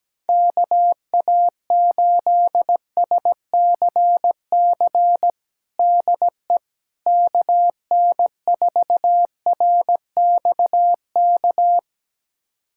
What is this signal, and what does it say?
Morse code, heard as a keyed tone, about 17 wpm, 700 Hz. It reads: KA8SCC DE KN4RXK
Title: USA Callsign DE Callsign 17wpm